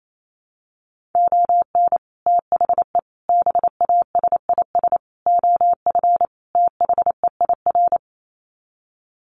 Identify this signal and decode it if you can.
Morse code, heard as a keyed tone, about 28 wpm, 700 Hz. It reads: ON THE BASIS OF THEIR
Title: Sets of 5 Words Encore 28wpm